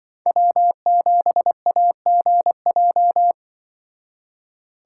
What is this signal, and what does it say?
Morse code, heard as a keyed tone, about 25 wpm, 700 Hz. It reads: W7AGJ
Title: Speed Racing - USA 1x3 Call Signs 20wpm